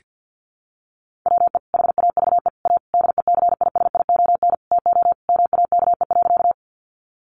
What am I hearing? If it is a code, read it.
Morse code, heard as a keyed tone, about 48 wpm, 700 Hz. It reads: WE HAVE A DECISION TO MAKE.
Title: Sentences from Top 1000 Words 50wpm